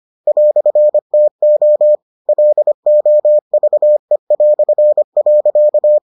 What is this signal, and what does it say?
Morse code, heard as a keyed tone, about 25 wpm, 600 Hz. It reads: "TO LOVE".